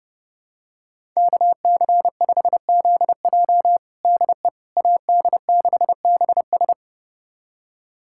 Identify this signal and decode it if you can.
Morse code, heard as a keyed tone, about 31 wpm, 700 Hz. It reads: KC5ZJ DE AD6BS
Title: USA Callsign DE Callsign - 5 Characters - 30wpm